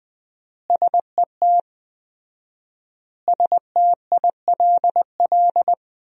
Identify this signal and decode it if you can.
Morse code, heard as a keyed tone, about 20 wpm, 700 Hz. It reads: SET STILL
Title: 4x Word-Spacing - Sentences from Top 300 Words - 20wpm